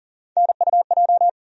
Morse code, heard as a keyed tone, about 40 wpm, 700 Hz. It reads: NUJ
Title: USA Call Sign Postfixes - 3 Letters 40wpm